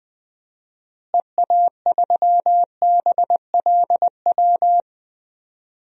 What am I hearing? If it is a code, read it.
Morse code, heard as a keyed tone, about 21 wpm, 700 Hz. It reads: EA3BLW
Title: DXCC-100 Call Signs - 6 Characters - Repeated 3x - 20wpm